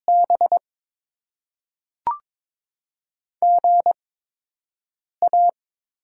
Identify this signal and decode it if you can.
Morse code, heard as a keyed tone, about 22 wpm, spaced as for 6 wpm, 700 Hz. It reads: B GA
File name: Top 20 QSO Elements - 8x Character Spacing - 22wpm